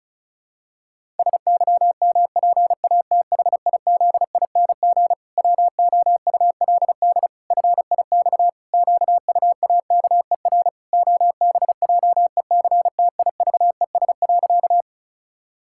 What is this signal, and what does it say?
Morse code, heard as a keyed tone, about 36 wpm, 700 Hz. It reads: SYMPATHIZING WOULD FIX QUAKER OBJECTIVES.
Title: Pangram Challenge v2 - 35wpm